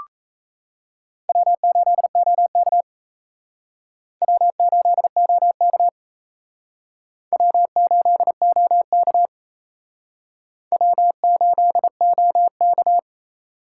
Morse code, heard as a keyed tone, about 37 wpm, 700 Hz. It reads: W8OK W8OK W8OK W8OK